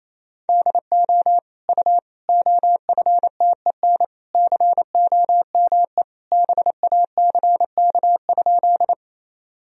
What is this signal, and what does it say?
Morse code, heard as a keyed tone, about 28 wpm, 700 Hz. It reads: DO U OFTEN COME BACK?
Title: Sentences from Top 300 Words and Abbreviations 28wpm